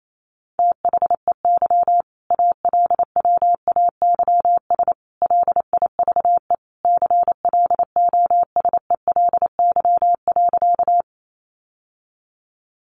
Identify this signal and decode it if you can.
Morse code, heard as a keyed tone, about 28 wpm, 700 Hz. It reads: THEY ALWAYS LIVE CLOSELY.
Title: Sentences from Top 1400 Words 28wpm